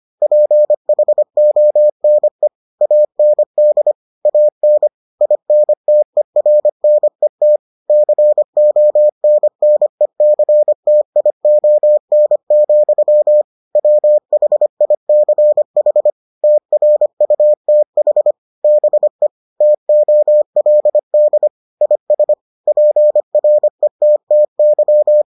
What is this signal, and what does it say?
Morse code, heard as a keyed tone, about 25 wpm, 600 Hz. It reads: PHONE AND AN INTERNET CONNECTION, WHICH TRUTH BE TOLD IS PRETTY